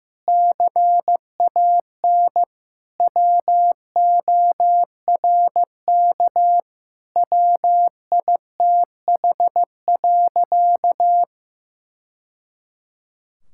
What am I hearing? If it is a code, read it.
Morse code, heard as a keyed tone, about 15 wpm, 700 Hz. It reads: CAN WORK WITH.
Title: Sentences from Top 1000 Words - Review Part 5 - 15wpm